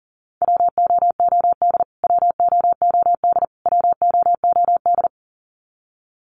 Morse code, heard as a keyed tone, about 40 wpm, 700 Hz. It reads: WOOD WOOD WOOD
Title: Top 500 Words Repeated 40wpm